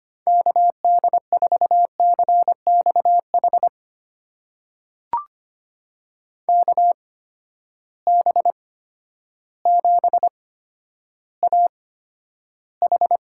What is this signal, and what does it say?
Morse code, heard as a keyed tone, about 25 wpm, spaced as for 11 wpm, 700 Hz. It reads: KD4CXH KB7AH
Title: USA Call Signs - 6 Characters - 8x Character Spacing - 25wpm